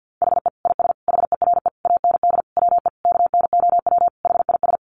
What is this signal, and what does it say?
Morse code, heard as a keyed tone, about 50 wpm, 700 Hz. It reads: HE IS HERE AND WE KNOW HIS
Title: Sentences from Top 800 Words 50wpm